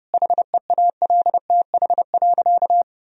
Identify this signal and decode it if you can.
Morse code, heard as a keyed tone, about 31 wpm, 700 Hz. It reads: HEALTH.
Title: Sentences from Top 1000 Words - Review Part 3 - 30wpm